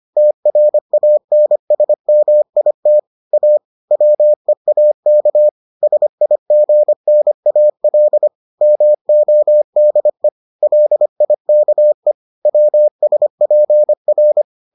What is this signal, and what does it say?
Morse code, heard as a keyed tone, about 25 wpm, 600 Hz. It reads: TRANSMIT A WEAK SIGNAL MODE LIKE WSPR